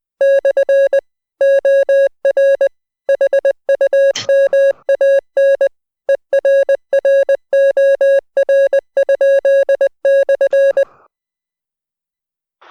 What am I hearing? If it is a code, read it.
Morse code, heard as a keyed tone, about 20 wpm, 550 Hz. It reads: / OR HUMAN ERROR?/